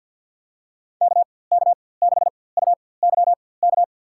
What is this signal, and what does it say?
Morse code, heard as a keyed tone, about 48 wpm, spaced as for 23 wpm, 700 Hz. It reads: KK6UYK
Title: USA Call Signs - 6 Characters - 4x Character Spacing - 50wpm